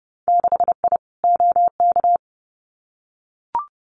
Morse code, heard as a keyed tone, about 30 wpm, 700 Hz. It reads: 6I OK
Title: Winter Field Day Exchanges Encore 30wpm